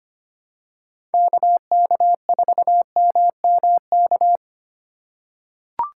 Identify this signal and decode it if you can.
Morse code, heard as a keyed tone, about 25 wpm, 700 Hz. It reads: KK4MMK
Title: USA Call Signs - 2x Character Spacing - 25wpm